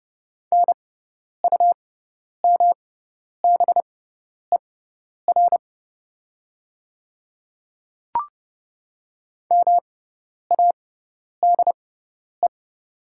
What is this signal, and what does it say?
Morse code, heard as a keyed tone, about 30 wpm, spaced as for 10 wpm, 700 Hz. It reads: NUMBER MADE
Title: 6x Character Spacing - Top 200 Words - 30wpm